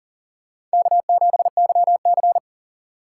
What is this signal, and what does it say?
Morse code, heard as a keyed tone, about 40 wpm, 700 Hz. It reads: K7YC
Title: USA Call Signs - 4 Characters - 2x Character Spacing - 40wpm